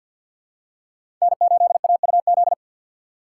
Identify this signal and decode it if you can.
Morse code, heard as a keyed tone, about 48 wpm, 700 Hz. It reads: N8AUZ
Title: USA Call Signs - 4x Character Spacing - 50wpm